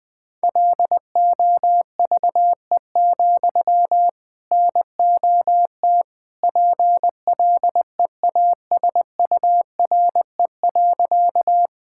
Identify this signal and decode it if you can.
Morse code, heard as a keyed tone, about 21 wpm, 700 Hz. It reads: LOVE, NOT PLEASURE.
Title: Sentences from Top 1600 Words 20wpm